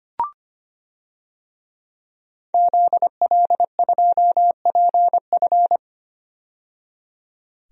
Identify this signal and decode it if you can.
Morse code, heard as a keyed tone, about 25 wpm, 700 Hz. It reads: ZL2PF
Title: DXCC-100 Call Signs - 5 Characters - 25wpm